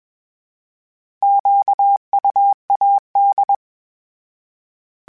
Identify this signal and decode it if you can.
Morse code, heard as a keyed tone, about 21 wpm, 800 Hz. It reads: QUAD
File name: fromSMs / FromSMs mp3s / Antennas@20